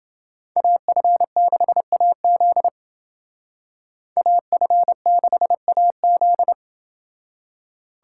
Morse code, heard as a keyed tone, about 29 wpm, 700 Hz. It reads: AF6AZ AF6AZ